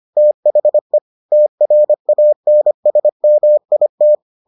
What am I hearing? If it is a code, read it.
Morse code, heard as a keyed tone, about 25 wpm, 600 Hz. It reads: THE TRANSMIT